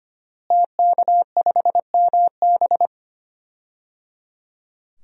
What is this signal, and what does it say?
Morse code, heard as a keyed tone, about 25 wpm, 700 Hz. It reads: TK5MB
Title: DXCC-100 Call Signs - 5 Characters - Repeated 3x - 25wpm